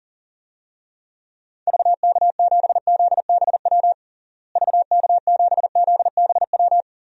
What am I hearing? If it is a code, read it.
Morse code, heard as a keyed tone, about 40 wpm, 700 Hz. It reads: VK7ZBW VK7ZBW
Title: DXCC-100 Call Signs - 6 Characters - Repeated 2x - 40wpm